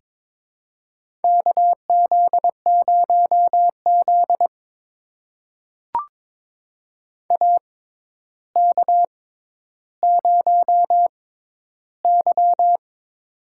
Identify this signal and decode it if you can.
Morse code, heard as a keyed tone, about 22 wpm, spaced as for 11 wpm, 700 Hz. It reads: KZ0Z AK0Y
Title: USA Call Signs - 4 Characters - 6x Character Spacing - 22wpm